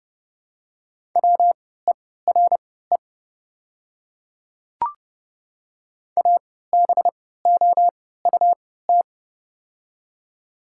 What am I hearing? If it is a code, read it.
Morse code, heard as a keyed tone, about 30 wpm, spaced as for 17 wpm, 700 Hz. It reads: WERE ABOUT
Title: 3x Character Spacing - Top 200 Words - 30wpm